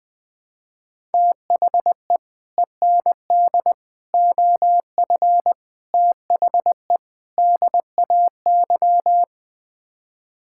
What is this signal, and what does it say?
Morse code, heard as a keyed tone, about 21 wpm, 700 Hz. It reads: THE END OF THE DAY
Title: Build A Phrase - 5-Word Phrases - 20wpm